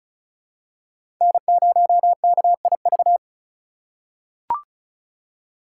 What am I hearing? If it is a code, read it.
Morse code, heard as a keyed tone, about 36 wpm, 700 Hz. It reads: N0KIV
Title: USA Call Signs - 5 Characters - 3x Character Spacing - 35wpm